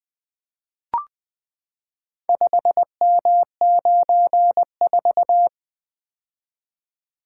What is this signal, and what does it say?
Morse code, heard as a keyed tone, about 20 wpm, 700 Hz. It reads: 5M94
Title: Four Letter-Number Combinations - 20wpm